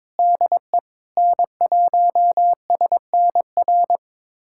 Morse code, heard as a keyed tone, about 22 wpm, 700 Hz. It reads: DE N1SNR
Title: USA Callsign DE Callsign - 5 Characters - 22wpm